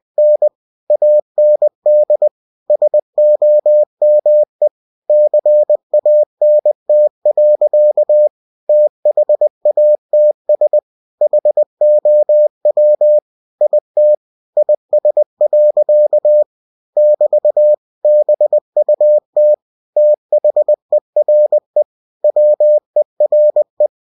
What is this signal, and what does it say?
Morse code, heard as a keyed tone, about 20 wpm, 600 Hz. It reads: N AND SOME CANT. THATS HOW IT IS. <BT> BUT THERE WERE
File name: cw-books / winnie-the-pooh / winnie-20wpm-page-0070